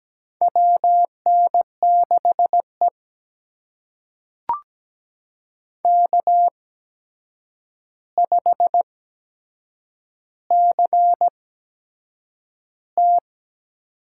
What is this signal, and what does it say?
Morse code, heard as a keyed tone, about 17 wpm, spaced as for 7 wpm, 700 Hz. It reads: WN6E K5CT
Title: USA Call Signs - 4 Characters - 8x Character Spacing - 17wpm